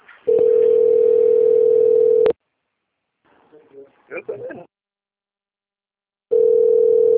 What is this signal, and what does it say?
Ringback tone (North America), 440 + 480 Hz, with speech.
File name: out- - - - - .6756